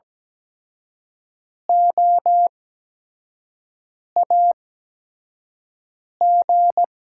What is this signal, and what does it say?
Morse code, heard as a keyed tone, about 17 wpm, spaced as for 5 wpm, 700 Hz. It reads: OAG
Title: USA Call Signs - 6 Characters - 8x Character Spacing - 17wpm